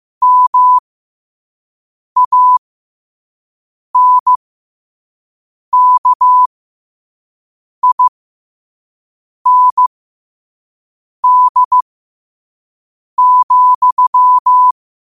Morse code, heard as a keyed tone, about 15 wpm, spaced as for 5 wpm, 1000 Hz. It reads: MANKIND,
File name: qotdfiles / 5wpm